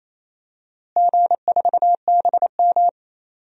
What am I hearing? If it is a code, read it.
Morse code, heard as a keyed tone, about 28 wpm, 700 Hz. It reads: G4BM
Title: DXCC-100 Call Signs - 4 Characters - Repeated 2x - 28wpm